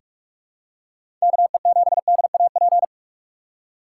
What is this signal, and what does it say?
Morse code, heard as a keyed tone, about 42 wpm, 700 Hz. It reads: KE7DAP
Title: USA Call Signs - 4x Character Spacing - 45wpm